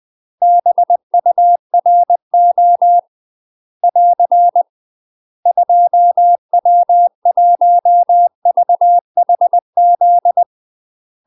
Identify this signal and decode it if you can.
Morse code, heard as a keyed tone, about 20 wpm, 700 Hz. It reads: BURO <AR> 2W1VHZ
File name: UK MASTER QSOs TRAINING / 20WPM / UK MASTER QSOs 0040